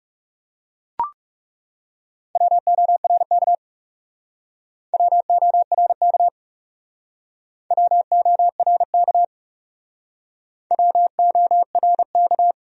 Morse code, heard as a keyed tone, about 39 wpm, 700 Hz. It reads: WORK WORK WORK WORK